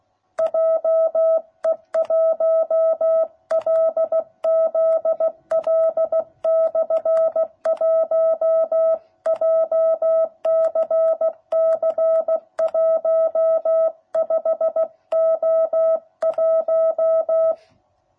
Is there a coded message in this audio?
Yes: Morse code from 0.4 to 17.5 s, heard as a keyed tone, about 16 wpm, 660 Hz. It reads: JE1LZL/1 JCC 15O1